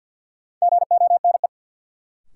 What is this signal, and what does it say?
Morse code, heard as a keyed tone, about 49 wpm, 700 Hz. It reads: GONE
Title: Sentences from Top 1200 Words - New Word Drill Alt - 50wpm